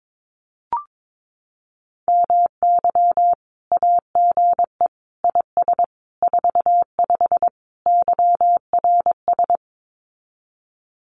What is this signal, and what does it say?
Morse code, heard as a keyed tone, about 22 wpm, 700 Hz. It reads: MY AGE IS 45 YRS
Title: Common QSO Phrases Encore 22wpm